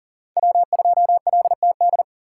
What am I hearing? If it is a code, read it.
Morse code, heard as a keyed tone, about 40 wpm, 700 Hz. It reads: W2LTD
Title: USA Call Signs - 5 Characters - 4x Character Spacing - 40wpm